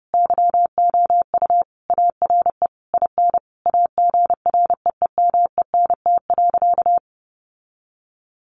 Morse code, heard as a keyed tone, about 30 wpm, 700 Hz. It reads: YOU ARE IN AGREEMENT.
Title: Sentences from Top 1500 Words 30wpm